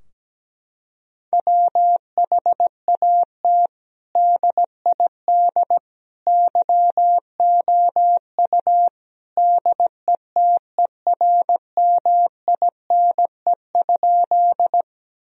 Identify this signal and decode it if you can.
Morse code, heard as a keyed tone, about 17 wpm, 700 Hz. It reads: WHAT DID YOU DETERMINE?
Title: Sentences from Top 1000 Words - Review Part 4 - 17wpm